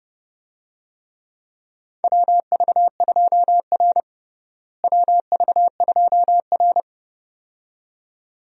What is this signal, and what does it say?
Morse code, heard as a keyed tone, about 30 wpm, 700 Hz. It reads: WV2R WV2R
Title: USA Call Signs - 4 Characters - Repeated 2x - 30wpm